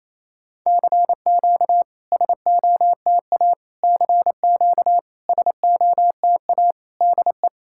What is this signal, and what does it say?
Morse code, heard as a keyed tone, about 27 wpm, 690 Hz. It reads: CQ SOTA CQ SOTA DE
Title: SOTA QSO Practice 28wpm